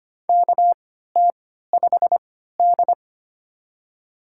Morse code, heard as a keyed tone, about 25 wpm, spaced as for 14 wpm, 700 Hz. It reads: KT5D
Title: USA Call Signs - 4 Characters - 3x Character Spacing - 25wpm